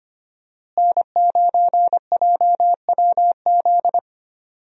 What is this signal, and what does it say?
Morse code, heard as a keyed tone, about 25 wpm, 700 Hz. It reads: N9JWZ
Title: USA Call Signs - 5 Characters - Repeated 2x - 25wpm